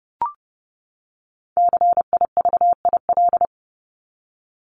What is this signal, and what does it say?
Morse code, heard as a keyed tone, about 30 wpm, 700 Hz. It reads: CIVIL
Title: Top 1751-2000 Words 30wpm